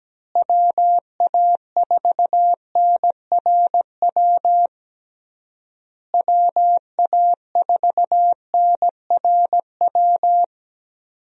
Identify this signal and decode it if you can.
Morse code, heard as a keyed tone, about 17 wpm, 700 Hz. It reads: WA4NRW WA4NRW
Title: USA Call Signs - 6 Characters - Repeated 3x - 17wpm